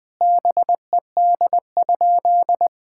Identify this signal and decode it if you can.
Morse code, heard as a keyed tone, about 20 wpm, 700 Hz. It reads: BED?
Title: Sentences from Top 1000 Words - Review Part 6 - 20wpm